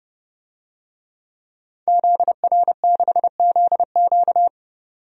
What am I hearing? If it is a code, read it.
Morse code, heard as a keyed tone, about 28 wpm, 690 Hz. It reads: ZR6ZQ
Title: DXCC-100 Call Signs - 5 Characters - 30wpm